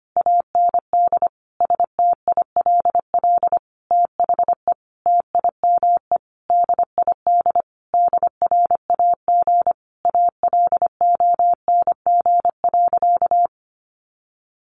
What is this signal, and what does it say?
Morse code, heard as a keyed tone, about 25 wpm, 700 Hz. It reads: AND STILL THE TIME DID DRAG ALONG.
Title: Sentences from Top 1500 Words 25wpm